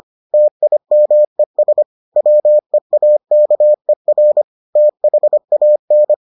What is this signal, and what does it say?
Morse code, heard as a keyed tone, about 25 wpm, 600 Hz. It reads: TIMES WEAKER THAN